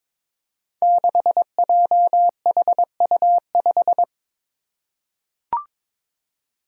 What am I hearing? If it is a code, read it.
Morse code, heard as a keyed tone, about 22 wpm, 700 Hz. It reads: 6JHU5
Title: Five Challenging Letter-Number Combinations 22wpm